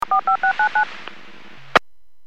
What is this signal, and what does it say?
DTMF, heard as telephone touch tones, tones of about 100 ms. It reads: 45699
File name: WPS / 03729 09-26 WPS 451.25 WPS